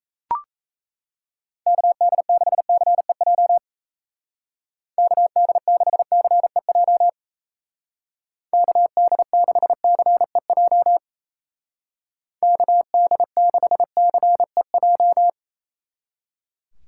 Morse code, heard as a keyed tone, about 35 wpm, 700 Hz. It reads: KD6CEJ KD6CEJ KD6CEJ KD6CEJ
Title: Speed Racing - USA 2x3 Call Signs 28wpm